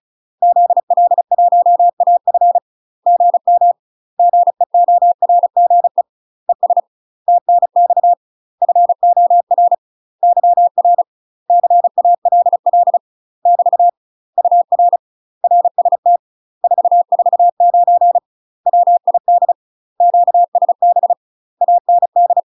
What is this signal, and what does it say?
Morse code, heard as a keyed tone, about 34 wpm, 700 Hz. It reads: ZL1AF GM GEORGE ES TNX FOR YR CALL <BT> UR RST 449 WID QSB AND
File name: UK MASTER QSOs TRAINING / 35WPM / UK MASTER QSOs 0018